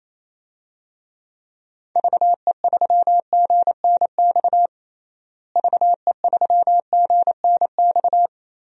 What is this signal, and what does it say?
Morse code, heard as a keyed tone, about 28 wpm, 700 Hz. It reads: VE3GNX VE3GNX
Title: DXCC-100 Call Signs - 6 Characters - Repeated 2x - 28wpm